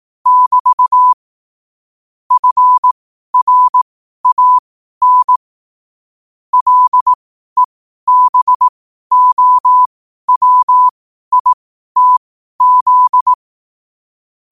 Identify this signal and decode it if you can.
Morse code, heard as a keyed tone, about 18 wpm, spaced as for 12 wpm, 1000 Hz. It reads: <BT> FRAN LEBOWITZ
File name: qotdfiles / 12wpm